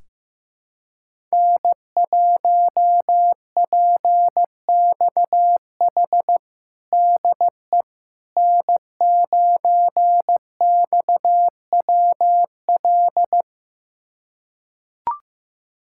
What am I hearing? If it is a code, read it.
Morse code, heard as a keyed tone, about 15 wpm, 700 Hz. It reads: N1PXH DE N9XWL
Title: USA Callsign DE Callsign - 5 Characters - 15wpm